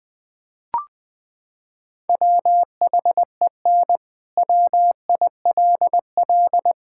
Morse code, heard as a keyed tone, about 20 wpm, 700 Hz. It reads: WHEN WILL
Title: Sentences from Top 1000 Words - Review Part 5 - 20wpm